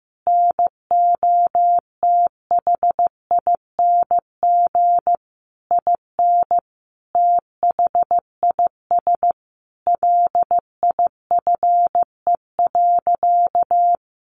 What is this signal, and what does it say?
Morse code, heard as a keyed tone, about 15 wpm, 700 Hz. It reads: NOTHING IN THIS LIFE.
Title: Sentences from Top 1200 Words 15wpm